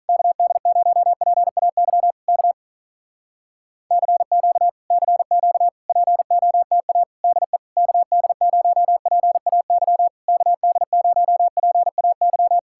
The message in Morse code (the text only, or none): KD0PAY K CQ CQ POTA DE KD0PAY KD0PAY